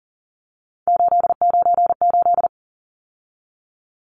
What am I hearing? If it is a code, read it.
Morse code, heard as a keyed tone, about 40 wpm, 700 Hz. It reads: 898
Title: Numbers - 3-Digit 40wpm